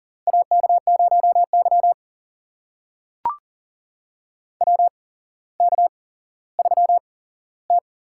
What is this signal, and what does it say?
Morse code, heard as a keyed tone, about 40 wpm, spaced as for 17 wpm, 700 Hz. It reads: AK0Y WK3T
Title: USA Call Signs - 4 Characters - 8x Character Spacing - 40wpm